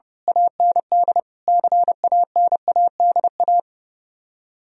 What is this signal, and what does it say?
Morse code, heard as a keyed tone, about 28 wpm, 700 Hz. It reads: AND CANADA
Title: Speed-Racing - Sets of 5 Words 30wpm